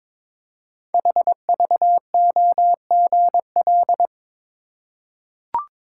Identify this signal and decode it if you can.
Morse code, heard as a keyed tone, about 21 wpm, 700 Hz. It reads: HVOGL
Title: Code Groups - 5 Letter:Numbers - 22wpm